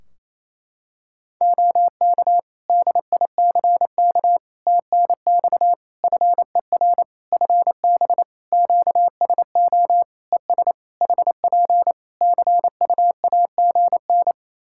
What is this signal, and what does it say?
Morse code, heard as a keyed tone, about 28 wpm, 700 Hz. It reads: OK DICK TNX FER FB QSO ES HP CUAGN
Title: Practice QSO - Exchange 4 - Ending - 28wpm